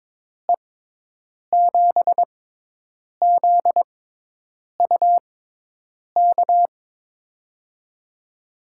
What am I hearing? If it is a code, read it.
Morse code, heard as a keyed tone, about 23 wpm, spaced as for 8 wpm, 700 Hz. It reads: E7ZUK
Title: USA Call Signs - 6 Characters - 6x Character Spacing - 22wpm